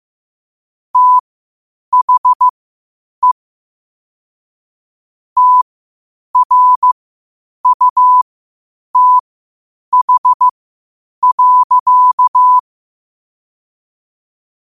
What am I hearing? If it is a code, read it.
Morse code, heard as a keyed tone, about 15 wpm, spaced as for 8 wpm, 1000 Hz. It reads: THE TRUTH.
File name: qotdfiles / 8wpm